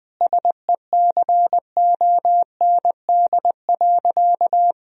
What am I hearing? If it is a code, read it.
Morse code, heard as a keyed tone, about 20 wpm, 700 Hz. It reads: SECOND.
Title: Sentences from Top 1000 Words - Review Part 1 - 20wpm